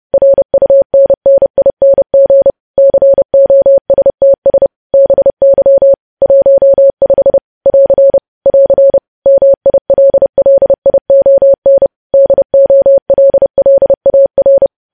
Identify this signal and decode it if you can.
Morse code, heard as a keyed tone, about 30 wpm, 550 Hz. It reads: RUNNING COSTS BY 15 <AR> <AR> MILLION DOLLAR
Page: news headlines in morse code at 30 wpm